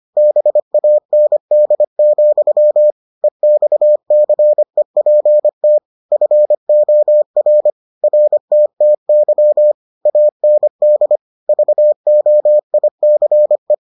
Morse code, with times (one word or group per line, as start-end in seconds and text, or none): 0.2-2.9 BAND,
3.2-5.8 EXCEPT
6.1-7.7 FOR
8.0-9.7 RTTY
10.1-11.2 AND
11.5-13.7 VOICE